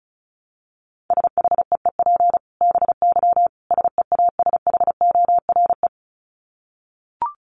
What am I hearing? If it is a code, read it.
Morse code, heard as a keyed tone, about 34 wpm, 700 Hz. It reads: SHEEP BY SEASHORE
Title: Dit Challenge 35wpm